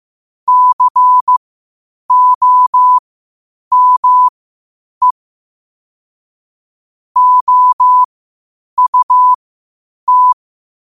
Morse code, heard as a keyed tone, about 15 wpm, spaced as for 8 wpm, 1000 Hz. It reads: COME OUT
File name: qotdfiles / 8wpm